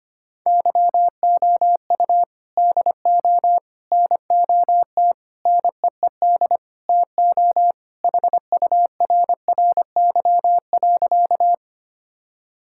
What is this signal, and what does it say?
Morse code, heard as a keyed tone, about 25 wpm, 700 Hz. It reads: YOU DO NOT NEED TO HURRY.
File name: practice / 25wpm / Sentences with 4-Word Phrases - 25wpm